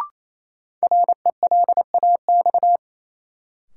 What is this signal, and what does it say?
Morse code, heard as a keyed tone, about 28 wpm, 700 Hz. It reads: RELAX
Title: Sentences from Top 1400 Words - New Word Drill - 28wpm